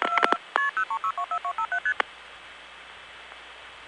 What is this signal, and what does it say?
Multi-frequency trunk signalling, heard as fast pairs of tones, tones of about 90 ms. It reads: KP03624257ST